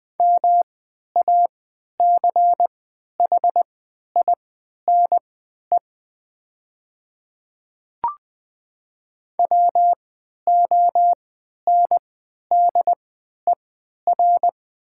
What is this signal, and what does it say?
Morse code, heard as a keyed tone, about 20 wpm, spaced as for 11 wpm, 700 Hz. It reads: MACHINE WONDER
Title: 3x Character Spacing - Top 500 Words - 20wpm